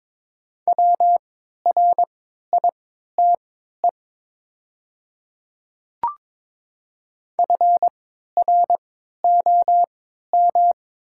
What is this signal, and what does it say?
Morse code, heard as a keyed tone, about 22 wpm, spaced as for 13 wpm, 700 Hz. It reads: WRITE FROM
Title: 3x Character Spacing - Top 100 Words - 22wpm